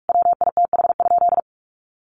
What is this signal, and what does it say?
Morse code, heard as a keyed tone, about 44 wpm, 700 Hz. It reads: WITH?
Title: Sentences from Top 1400 Words 45wpm